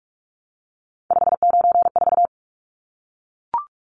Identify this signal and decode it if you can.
Morse code, heard as a keyed tone, about 44 wpm, 700 Hz. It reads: 594
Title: Numbers - 3-Digit 45wpm